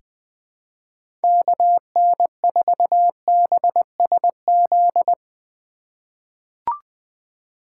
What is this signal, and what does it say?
Morse code, heard as a keyed tone, about 21 wpm, 700 Hz. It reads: KN4BSZ
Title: USA Call Signs - 6 Characters - 3x Character Spacing - 20wpm